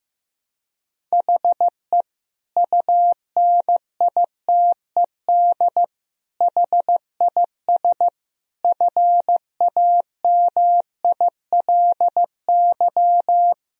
Morse code, heard as a keyed tone, about 15 wpm, 700 Hz. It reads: HE UNITED HIS FAMILY
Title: Sentences from Top 1000 Words - Review Part 4 - 15wpm